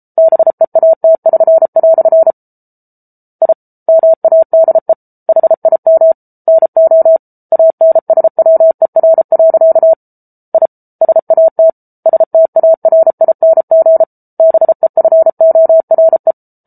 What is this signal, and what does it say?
Morse code, heard as a keyed tone, about 34 wpm, 650 Hz. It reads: BEAT<SN>" I MADE HIM NO ANSWER. I SAT STARING BEFORE